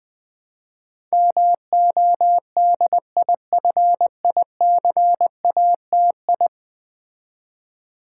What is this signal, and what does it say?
Morse code, heard as a keyed tone, about 20 wpm, 700 Hz. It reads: MODIFICATI
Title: Build A Word - 5 Letters At A Time - 12-Letter Words - 20wpm